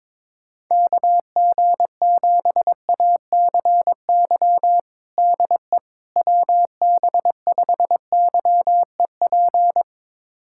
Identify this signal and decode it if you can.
Morse code, heard as a keyed tone, about 22 wpm, 700 Hz. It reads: KG7ACY DE WB5YEP
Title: USA Callsign DE Callsign - 6 Characters - 22wpm